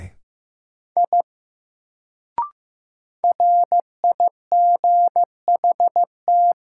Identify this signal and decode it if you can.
Morse code, heard as a keyed tone, about 15 wpm, 700 Hz. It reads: I RIGHT
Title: Sentences from Top 200 Words - New Word Drill - 15wpm